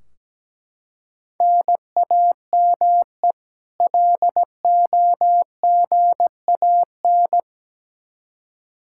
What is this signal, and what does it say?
Morse code, heard as a keyed tone, about 17 wpm, 700 Hz. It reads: NAME LOGAN
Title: Practice QSO - Exchange 1 - Beginning - 17wpm